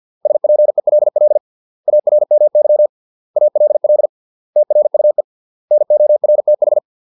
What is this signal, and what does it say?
Morse code, heard as a keyed tone, about 50 wpm, 600 Hz. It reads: SPELL ARMY ALL TRUE NORTH